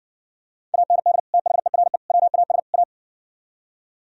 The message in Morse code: AND THERE WAS A

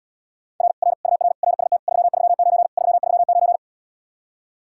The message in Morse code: R R UR RST 339 339

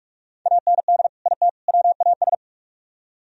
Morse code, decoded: AND IT WAS

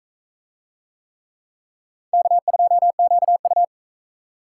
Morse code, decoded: K2QU

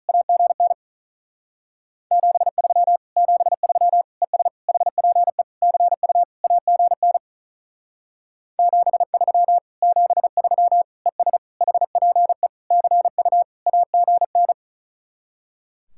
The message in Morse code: AGN 73 73 ES HPE CU AGN 73 73 ES HPE CU AGN

